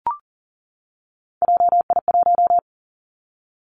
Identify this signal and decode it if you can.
Morse code, heard as a keyed tone, about 39 wpm, 700 Hz. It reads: JI1
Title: Three Challenging Letter-Number Combinations 40wpm